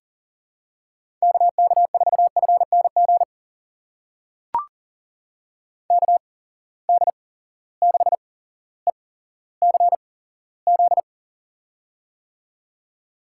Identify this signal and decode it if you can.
Morse code, heard as a keyed tone, about 40 wpm, spaced as for 17 wpm, 700 Hz. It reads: KK4FNG KD6ECZ
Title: USA Call Signs - 6 Characters - 8x Character Spacing - 40wpm